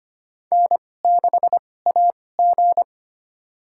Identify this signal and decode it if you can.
Morse code, heard as a keyed tone, about 25 wpm, spaced as for 18 wpm, 700 Hz. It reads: N6AG
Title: USA Call Signs - 4 Characters - 2x Character Spacing - 25wpm